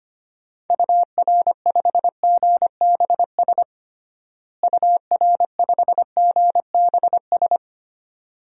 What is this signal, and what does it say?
Morse code, heard as a keyed tone, about 25 wpm, 700 Hz. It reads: UR5GBS UR5GBS
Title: DXCC-100 Call Signs - 6 Characters - Repeated 2x - 25wpm